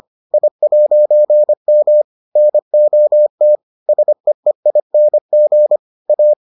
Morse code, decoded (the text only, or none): I'M NOT SEEING A